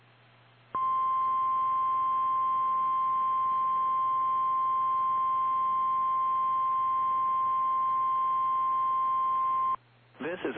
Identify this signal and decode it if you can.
Emergency Alert System (SAME) tones (the 1050 Hz attention tone), with speech. It is heard from 0.7 to 9.8 s.
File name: eas / KKPRKXPN / KKPRKXPN